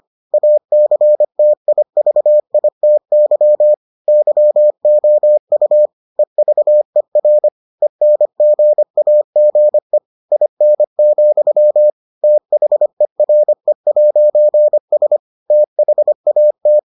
Morse code, heard as a keyed tone, about 25 wpm, 600 Hz. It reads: ACTIVITY YOU EVER ENGAGE IN, THERE'S THAT